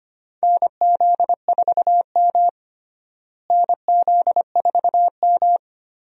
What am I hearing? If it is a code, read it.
Morse code, heard as a keyed tone, about 25 wpm, 700 Hz. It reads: NZ4M NZ4M
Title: USA Call Signs - 4 Characters - Repeated 2x - 25wpm